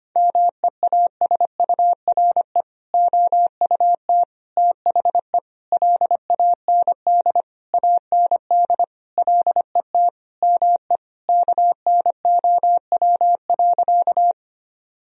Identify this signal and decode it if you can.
Morse code, heard as a keyed tone, about 25 wpm, 700 Hz. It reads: MEASURE OUT THE LAND AND LET ME KNOW.
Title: Sentences from Top 1000 Words - Review Part 2 - 25wpm